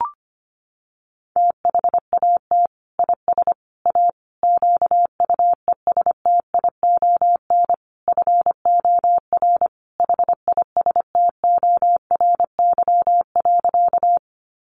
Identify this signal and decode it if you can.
Morse code, heard as a keyed tone, about 25 wpm, 700 Hz. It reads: THAT IS A QUESTION FOR HISTORY.
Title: Sentences from Top 600 Words 25wpm